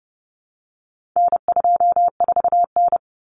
Morse code, heard as a keyed tone, about 30 wpm, 700 Hz. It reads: N24N
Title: Common Cut Numbers - 4-Digit 30wpm